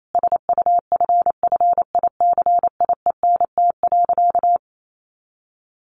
Morse code, heard as a keyed tone, about 28 wpm, 700 Hz. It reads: SUFFICIENT.
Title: Sentences from Top 1400 Words 28wpm